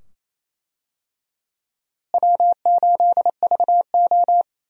Morse code, heard as a keyed tone, about 28 wpm, 700 Hz. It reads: W8VO
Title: USA Call Signs - 4 Characters - Repeated 2x - 28wpm